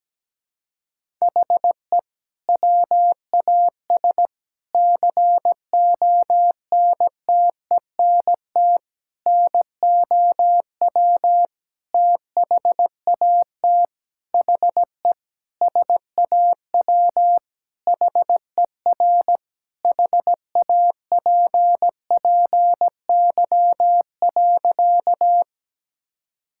Morse code, heard as a keyed tone, about 17 wpm, 700 Hz. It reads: HE WAS CONTENT NOW THAT HE SAW HER HAPPY.
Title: Sentences from Top 1000 Words - Review Part 6 - 17wpm